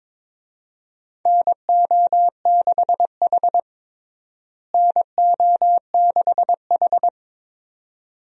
Morse code, heard as a keyed tone, about 21 wpm, 700 Hz. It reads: NO6H NO6H
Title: USA Call Signs - 4 Characters - Repeated 2x - 22wpm